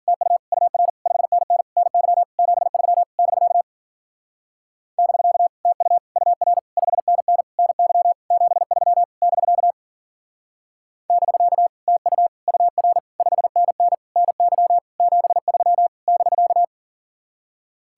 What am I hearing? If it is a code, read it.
Morse code, heard as a keyed tone, about 52 wpm, 700 Hz. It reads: TU UR 5NN NY 73 <BK> <BK> TU UR 5NN NY 73 <BK> <BK> TU UR 5NN NY 73 <BK>